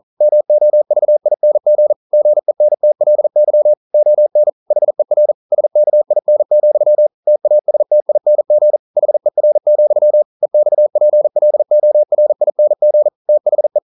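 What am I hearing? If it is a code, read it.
Morse code, heard as a keyed tone, about 40 wpm, 600 Hz. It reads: MOVING GENTLY ON HER SKIN, TASTING HER, EXPLORING THE
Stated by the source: Morse code practice at 41 wpm, from CW med Gnister.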